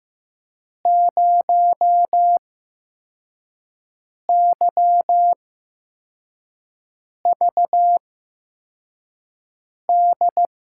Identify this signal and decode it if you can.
Morse code, heard as a keyed tone, about 15 wpm, spaced as for 4 wpm, 700 Hz. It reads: 0YVD
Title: USA Call Signs - 6 Characters - 8x Character Spacing - 15wpm